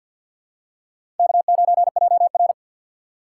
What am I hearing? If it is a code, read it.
Morse code, heard as a keyed tone, about 51 wpm, 700 Hz. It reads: K9JR
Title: USA Call Signs - 4 Characters - 3x Character Spacing - 50wpm